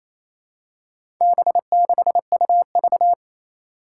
Morse code, heard as a keyed tone, about 28 wpm, 700 Hz. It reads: B6UV
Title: Four Challenging Letter-Number Combinations 28wpm